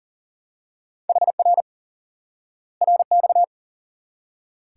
Morse code, heard as a keyed tone, about 40 wpm, 700 Hz. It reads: HR RX
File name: IQE Course 3 - Lesson 89 - YAGI - Two Elements - 40wpm